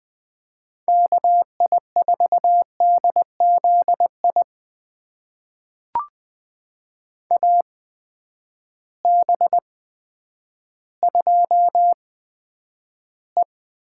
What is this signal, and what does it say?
Morse code, heard as a keyed tone, about 20 wpm, spaced as for 10 wpm, 700 Hz. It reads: KI4DZI AB2E
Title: USA Call Signs - 8x Character Spacing - 20wpm